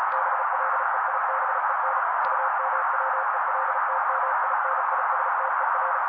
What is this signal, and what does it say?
Morse code, heard as a keyed tone, about 35 wpm, 530 Hz. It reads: BLEIBEN UMWELTMINISTER
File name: daily / 35wpm noise